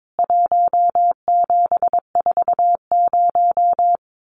Morse code, heard as a keyed tone, about 22 wpm, 700 Hz. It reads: 1740
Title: CWT Exchange 22wpm